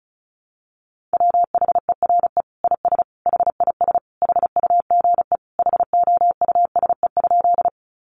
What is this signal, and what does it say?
Morse code, heard as a keyed tone, about 34 wpm, 700 Hz. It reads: WHERE IS HIS HUGE HOUSE?
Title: Sentences from Top 1100 Words 35wpm